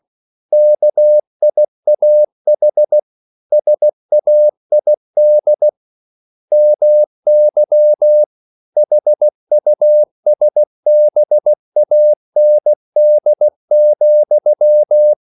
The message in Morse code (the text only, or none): KIAH SAID MY HUSBAND,